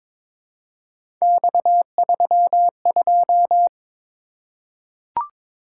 Morse code, heard as a keyed tone, about 23 wpm, 700 Hz. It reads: X32
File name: practice / 22wpm / Three Letter-Number Combinations - 22wpm